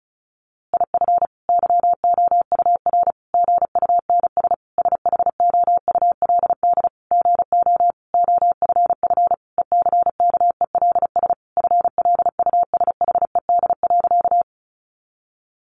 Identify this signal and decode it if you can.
Morse code, heard as a keyed tone, about 34 wpm, 700 Hz. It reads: IF YOUR GUNS SHOULD GO OFF ECKELS FLUSHED.
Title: A Sound Of Thunder 35wpm